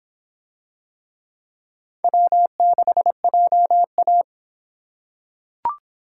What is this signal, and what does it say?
Morse code, heard as a keyed tone, about 26 wpm, 700 Hz. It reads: W6JA